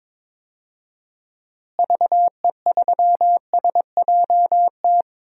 Morse code, heard as a keyed tone, about 22 wpm, 700 Hz. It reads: VE3SJT
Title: DXCC-100 Call Signs - 6 Characters - Repeated 3x - 22wpm